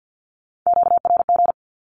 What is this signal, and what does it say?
Morse code, heard as a keyed tone, about 48 wpm, 700 Hz. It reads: QRG
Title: Q-Codes 50wpm